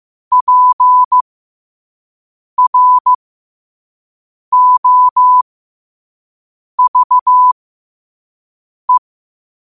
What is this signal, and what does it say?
Morse code, heard as a keyed tone, about 15 wpm, spaced as for 5 wpm, 1000 Hz. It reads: PROVE
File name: qotdfiles / 5wpm